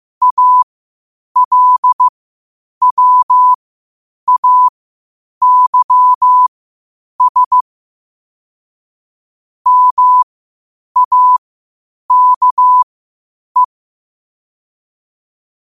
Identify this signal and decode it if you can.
Morse code, heard as a keyed tone, about 15 wpm, spaced as for 8 wpm, 1000 Hz. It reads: ALWAYS MAKE
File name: qotdfiles / 8wpm